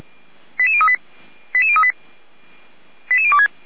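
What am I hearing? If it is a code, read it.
Five-tone selective call (ZVEI), heard as quick runs of five tones, tones of about 70 ms. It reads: 88038 88038 88026
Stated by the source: ZVEI five-tone selective call 08946 received on an SDR by feuerwehr-orlamuende, a fire brigade at Orlamünde.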